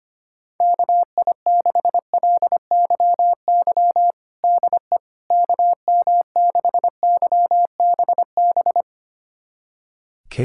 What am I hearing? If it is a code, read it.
Morse code, heard as a keyed tone, about 25 wpm, 700 Hz. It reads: KI6LYY DE KM6YBB
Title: USA Callsign DE Callsign - 6 Characters - 25wpm